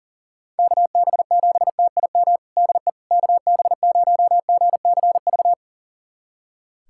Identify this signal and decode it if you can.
Morse code, heard as a keyed tone, about 40 wpm, 690 Hz. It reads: KB7TIM DE KB0GCV
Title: USA Callsign DE Callsign - 6 Characters - 40wpm